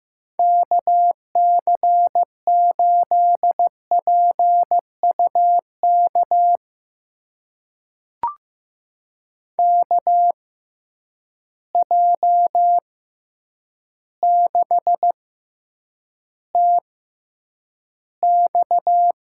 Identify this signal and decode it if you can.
Morse code, heard as a keyed tone, about 15 wpm, spaced as for 8 wpm, 700 Hz. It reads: KC8PUK KJ6TX
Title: USA Call Signs - 6x Character Spacing - 15wpm